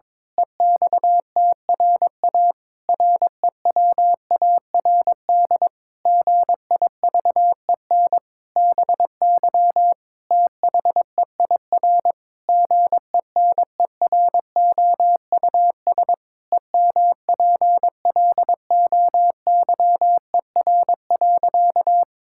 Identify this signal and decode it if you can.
Morse code, heard as a keyed tone, about 22 wpm, 700 Hz. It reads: EXTRA REWARD GIVEN BY THEIR GENEROUS EMPLOYER.
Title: Pangram Challenge v2 - 22wpm